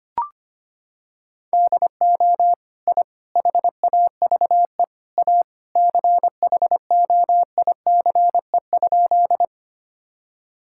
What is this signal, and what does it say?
Morse code, heard as a keyed tone, about 25 wpm, 700 Hz. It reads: DO I HAVE A CHOICE?
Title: Sentences from Top 1000 Words - Review Part 6 - 25wpm